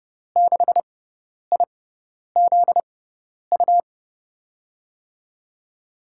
Morse code, heard as a keyed tone, about 30 wpm, spaced as for 10 wpm, 700 Hz. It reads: 6IZU
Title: USA Call Signs - 5 Characters - 6x Character Spacing - 30wpm